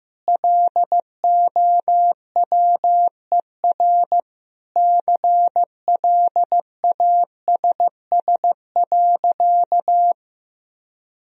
Morse code, heard as a keyed tone, about 15 wpm, 700 Hz. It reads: LOWER CLASS.
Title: Sentences from Top 1000 Words - Review Part 4 - 15wpm